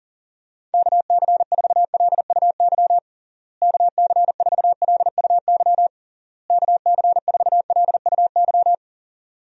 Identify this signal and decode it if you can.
Morse code, heard as a keyed tone, about 40 wpm, 700 Hz. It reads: KC4LUY KC4LUY KC4LUY
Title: USA Call Signs - 6 Characters - Repeated 3x - 40wpm